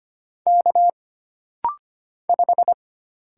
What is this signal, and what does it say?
Morse code, heard as a keyed tone, about 25 wpm, 700 Hz. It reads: K 5
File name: Single Character (Letter:Number:Punctuation:Prosign) - Rapid-Fire - 25wpm